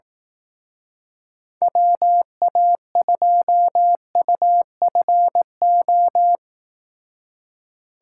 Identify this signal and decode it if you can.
Morse code, heard as a keyed tone, about 18 wpm, 700 Hz. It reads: WA2UFO
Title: Speed Racing - USA 2x3 Call Signs 15wpm